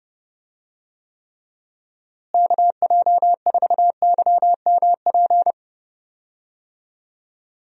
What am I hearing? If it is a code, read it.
Morse code, heard as a keyed tone, about 31 wpm, 700 Hz. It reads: KJ4YMP
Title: USA Call Signs - 6 Characters - Repeated 3x - 30wpm